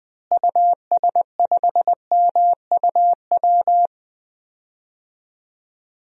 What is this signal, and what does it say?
Morse code, heard as a keyed tone, about 19 wpm, 700 Hz. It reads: US5MUW
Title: DXCC-100 Call Signs - 6 Characters - Repeated 3x - 20wpm